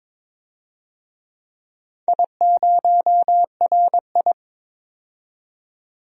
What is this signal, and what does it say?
Morse code, heard as a keyed tone, about 22 wpm, 700 Hz. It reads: I0RI
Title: DXCC-100 Call Signs - 4 Characters - 22wpm